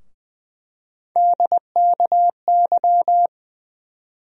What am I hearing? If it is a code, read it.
Morse code, heard as a keyed tone, about 20 wpm, 700 Hz. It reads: DKY
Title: USA Call Sign Postfixes - 3 Letters 20wpm